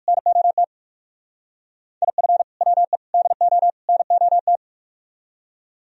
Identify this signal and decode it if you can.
Morse code, heard as a keyed tone, about 55 wpm, 700 Hz. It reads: NOT IF WE DO NOT
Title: Speed-Racing - Sets of 4 Words 45wpm